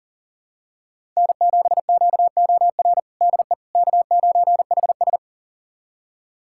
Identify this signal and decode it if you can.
Morse code, heard as a keyed tone, about 37 wpm, 700 Hz. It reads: N7QOR DE K9HS
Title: USA Callsign DE Callsign 40wpm